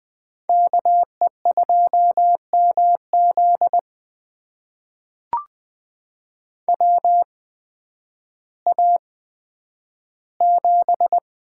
Morse code, heard as a keyed tone, about 20 wpm, spaced as for 11 wpm, 700 Hz. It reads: KE2MZ WA7
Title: USA Call Signs - 8x Character Spacing - 20wpm